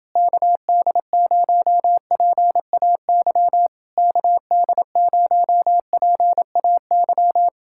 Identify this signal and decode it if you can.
Morse code, heard as a keyed tone, about 29 wpm, 700 Hz. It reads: KD0PAY KD0PAY